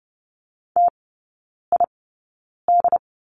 Morse code, heard as a keyed tone, about 30 wpm, spaced as for 9 wpm, 690 Hz. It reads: TID
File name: Lesson 16e - Letter c - Three Characters 30wpm